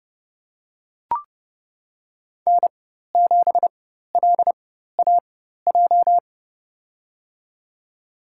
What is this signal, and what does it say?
Morse code, heard as a keyed tone, about 31 wpm, spaced as for 14 wpm, 700 Hz. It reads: N7LAJ
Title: USA Call Signs - 5 Characters - 4x Character Spacing - 30wpm